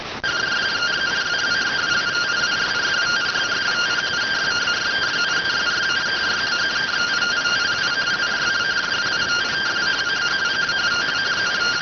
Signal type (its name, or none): two-tone teleprinter signal